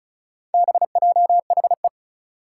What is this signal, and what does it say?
Morse code, heard as a keyed tone, about 36 wpm, 700 Hz. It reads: BJHE
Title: Four Challenging Letter-Number Combinations 35wpm